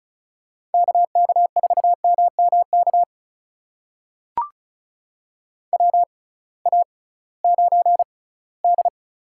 Morse code, heard as a keyed tone, about 34 wpm, spaced as for 20 wpm, 700 Hz. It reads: KK4MMK WA9D
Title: USA Call Signs - 6x Character Spacing - 35wpm